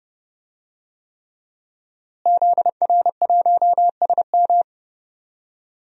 Morse code, heard as a keyed tone, about 30 wpm, 700 Hz. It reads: ZR1SM
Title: DXCC-100 Call Signs - 5 Characters - Repeated 3x - 30wpm